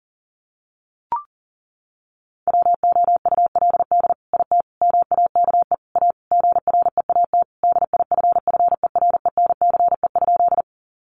Morse code, heard as a keyed tone, about 40 wpm, 700 Hz. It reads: WOULD IT MAKE A GREAT DIFFERENCE?
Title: Sentences from Top 800 Words 40wpm